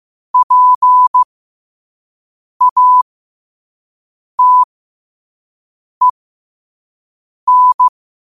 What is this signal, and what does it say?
Morse code, heard as a keyed tone, about 15 wpm, spaced as for 5 wpm, 1000 Hz. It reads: PATEN
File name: qotdfiles / 5wpm